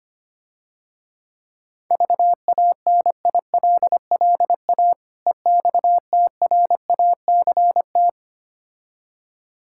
Morse code, heard as a keyed tone, about 26 wpm, 700 Hz. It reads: VANILLA EXTRACT
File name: practice / 25wpm / Difficult Words - Speed-Racing with Hints - 25wpm